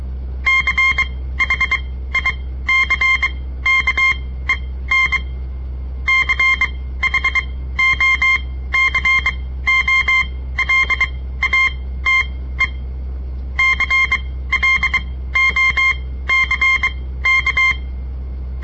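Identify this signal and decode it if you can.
Morse code, heard as a keyed tone, about 23 wpm, spaced as for 15 wpm, 2100 Hz. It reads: CHICKEN CHOCOLATE CLOCK
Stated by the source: On-air Morse code practice broadcast by a southern Montana amateur radio association; all code, no read back.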